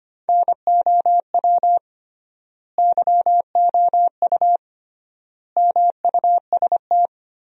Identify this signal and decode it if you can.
Morse code, heard as a keyed tone, about 25 wpm, 700 Hz. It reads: NOW YOU MUST
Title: 3x Word-Spacing - Sentences from Top 300 Words - 25wpm